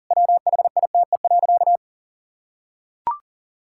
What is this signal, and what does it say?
Morse code, heard as a keyed tone, about 40 wpm, 700 Hz. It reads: WHITE.
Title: Pangram Challenge v2 - 40wpm